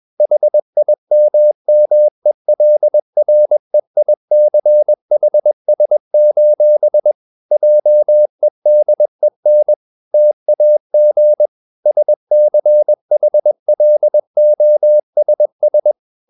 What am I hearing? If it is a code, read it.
Morse code, heard as a keyed tone, about 21 wpm, 600 Hz. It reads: HIMMELREICHS: JEDEN TAG SCHLOSS